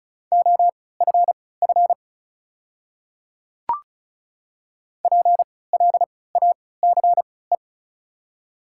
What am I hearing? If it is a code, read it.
Morse code, heard as a keyed tone, about 36 wpm, spaced as for 20 wpm, 700 Hz. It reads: OFF PLACE
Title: 3x Character Spacing - Top 200 Words - 35wpm